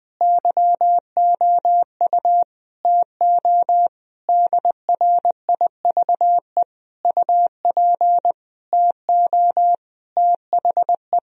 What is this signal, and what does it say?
Morse code, heard as a keyed tone, about 20 wpm, 700 Hz. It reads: YOU TO DRIVE UP TO THE
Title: Sentences from Top 1000 Words - Review Part 2 - 20wpm